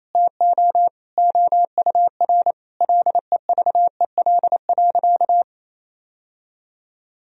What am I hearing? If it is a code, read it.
Morse code, heard as a keyed tone, about 28 wpm, 700 Hz. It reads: TO OUR LEVEL.
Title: Sentences from Top 1000 Words - Review Part 4 - 28wpm